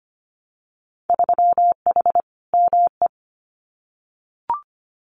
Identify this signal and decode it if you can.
Morse code, heard as a keyed tone, about 25 wpm, 700 Hz. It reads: 3H ME
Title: Winter Field Day Exchanges Encore 25wpm